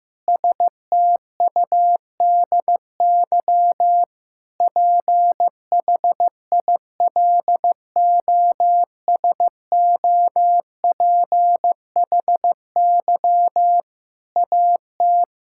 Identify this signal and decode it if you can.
Morse code, heard as a keyed tone, about 15 wpm, 700 Hz. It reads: STUDY PHILOSOPHY AT
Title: Sentences from Top 1000 Words - Review Part 6 - 15wpm